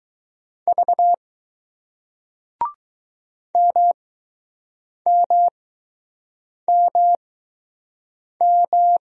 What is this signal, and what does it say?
Morse code, heard as a keyed tone, about 20 wpm, spaced as for 6 wpm, 700 Hz. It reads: V MMMM